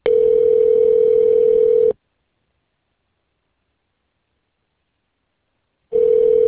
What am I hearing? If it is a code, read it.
Ringback tone (North America), 440 + 480 Hz.